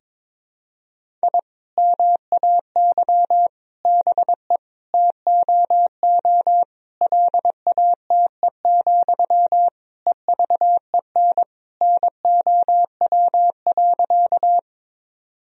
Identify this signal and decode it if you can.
Morse code, heard as a keyed tone, about 21 wpm, 700 Hz. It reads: I MAY BE TOO LATE, EVEN NOW.
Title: Sentences from Top 1000 Words - Review Part 1 - 22wpm